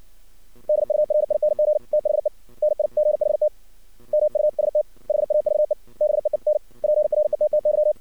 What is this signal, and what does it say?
Morse code, heard as a keyed tone, about 59 wpm, 610 Hz. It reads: DURING THE NIGHT MUST HAVE BEEN PRETTY
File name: archived / CRSnet